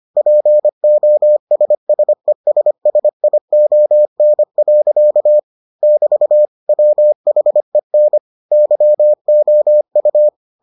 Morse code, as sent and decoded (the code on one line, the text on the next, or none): .--. --- ... ... ... ... .. --- -. .-.-.- / -...- / .-- .... -. / -.-- --- ..-
POSSESSION. <BT> WHEN YOU